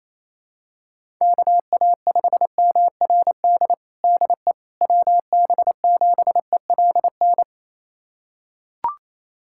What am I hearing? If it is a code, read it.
Morse code, heard as a keyed tone, about 28 wpm, 700 Hz. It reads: KA5MRD DE WB7ELN
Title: USA Callsign DE Callsign 28wpm